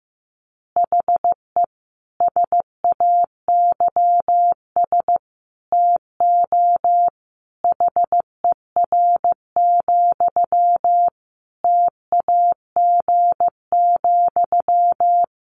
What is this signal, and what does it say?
Morse code, heard as a keyed tone, about 15 wpm, 700 Hz. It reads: HE SAYS TO HER, TAG,
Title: Additional Easy Phrases 15wpm